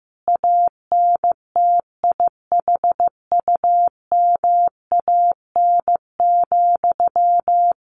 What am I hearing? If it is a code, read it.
Morse code, heard as a keyed tone, about 15 wpm, 700 Hz. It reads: ANTIHUMAN,
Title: A Sound Of Thunder 15wpm